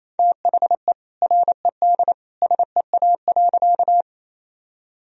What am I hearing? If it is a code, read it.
Morse code, heard as a keyed tone, about 28 wpm, 700 Hz. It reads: THE RED SEA.
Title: Sentences from Top 1000 Words - Review Part 2 - 28wpm